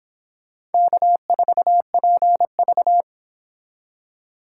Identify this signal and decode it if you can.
Morse code, heard as a keyed tone, about 30 wpm, 700 Hz. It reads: K4PV